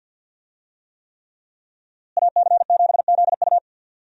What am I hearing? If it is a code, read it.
Morse code, heard as a keyed tone, about 52 wpm, 700 Hz. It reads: AC7ZU
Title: USA Call Signs - 5 Characters - Repeated 3x - 50wpm